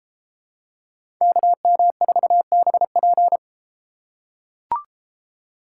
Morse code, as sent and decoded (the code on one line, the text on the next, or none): -.- -- ....- -... .--.
KM4BP